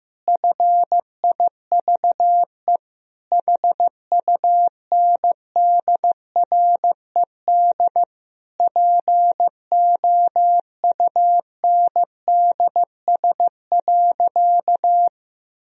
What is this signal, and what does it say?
Morse code, heard as a keyed tone, about 15 wpm, 700 Hz. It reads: FIVE HUNDRED POUNDS.
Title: Sentences from Top 1000 Words - Review Part 4 - 15wpm